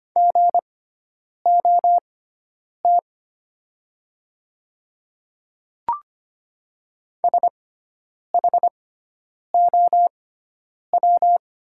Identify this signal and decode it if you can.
Morse code, heard as a keyed tone, about 25 wpm, spaced as for 9 wpm, 700 Hz. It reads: GOT SHOW
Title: 6x Character Spacing - Top 300 Words - 25wpm